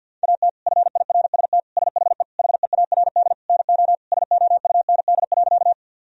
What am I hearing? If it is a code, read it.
Morse code, heard as a keyed tone, about 46 wpm, 700 Hz. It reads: AT FIRST SHE HEARD NO SOUND.
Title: Sentences from Top 1000 Words - Review Part 1 - 50wpm